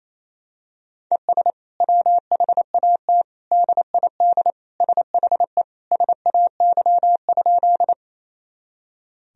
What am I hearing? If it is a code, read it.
Morse code, heard as a keyed tone, about 28 wpm, 700 Hz. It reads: ES WHAT DID SHE SAY?
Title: Sentences from Top 200 Words and Abbreviations 28wpm